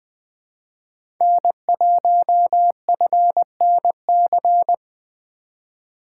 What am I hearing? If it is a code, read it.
Morse code, heard as a keyed tone, about 20 wpm, 700 Hz. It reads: N1FNC
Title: USA Call Signs - 5 Characters - 2x Character Spacing - 20wpm